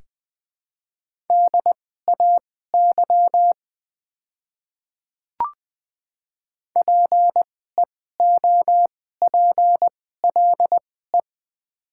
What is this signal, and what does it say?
Morse code, heard as a keyed tone, about 20 wpm, spaced as for 15 wpm, 700 Hz. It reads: DAY PEOPLE
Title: 2x Character Spacing - Top 100 Words - 20wpm